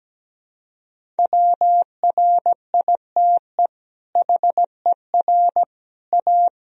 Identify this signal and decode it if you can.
Morse code, heard as a keyed tone, about 17 wpm, 700 Hz. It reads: WRITE HER A
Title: Sentences from Top 1000 Words - Review Part 1 - 17wpm